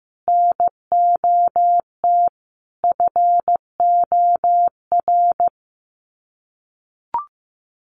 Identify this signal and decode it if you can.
Morse code, heard as a keyed tone, about 15 wpm, 700 Hz. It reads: NOT FOR
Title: Sets of 5 Words Encore 15wpm